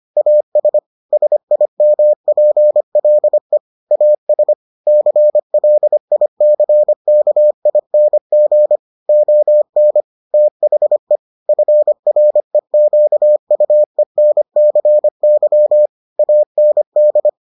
Morse code, heard as a keyed tone, about 25 wpm, 600 Hz. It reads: AS SIMPLE AS CLICKING ON THE FREQUENCY AND